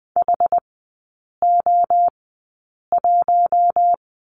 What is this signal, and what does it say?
Morse code, heard as a keyed tone, about 20 wpm, spaced as for 8 wpm, 700 Hz. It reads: HO1
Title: 107 Lesson 18e - Letter m - Three Characters 20wpm